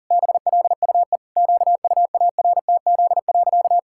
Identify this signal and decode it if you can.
Morse code, heard as a keyed tone, about 40 wpm, 700 Hz. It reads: BLUE QUARTZ.
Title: Pangram Challenge v2 - 40wpm